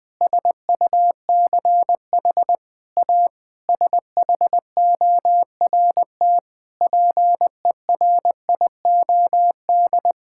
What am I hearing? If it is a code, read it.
Morse code, heard as a keyed tone, about 20 wpm, 700 Hz. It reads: SUCH A SHORT PERIOD
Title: Sentences with 7-Word Phrases - Spoken Beginning - 20wpm